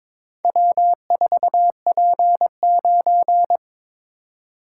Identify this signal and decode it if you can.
Morse code, heard as a keyed tone, about 23 wpm, 690 Hz. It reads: W4P9
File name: practice / 22wpm / Four Letter-Number Combinations - 22wpm